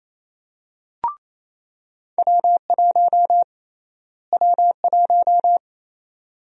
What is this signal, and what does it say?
Morse code, heard as a keyed tone, about 28 wpm, 700 Hz. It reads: W1 W1
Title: USA Call Sign Prefixes - Repeated 2x - 28wpm